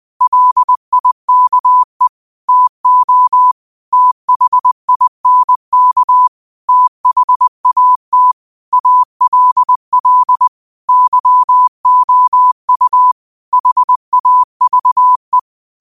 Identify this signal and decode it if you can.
Morse code, heard as a keyed tone, about 20 wpm, 1000 Hz. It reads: LIKE TO THINK THAT ALL YOU HAVE